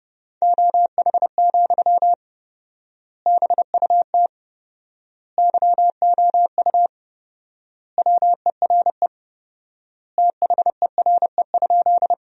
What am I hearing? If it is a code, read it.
Morse code, heard as a keyed tone, about 30 wpm, 700 Hz. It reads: OH, BUT YOU WERE THERE?
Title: 4x Word-Spacing - Sentences from Top 500 Words - 30wpm